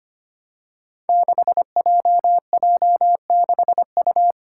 Morse code, heard as a keyed tone, about 25 wpm, 700 Hz. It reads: 6JJ6U
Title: Five Challenging Letter-Number Combinations 25wpm